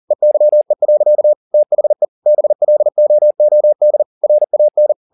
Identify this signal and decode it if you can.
Morse code, heard as a keyed tone, about 40 wpm, 600 Hz. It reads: EYE. THE BLOOD RAN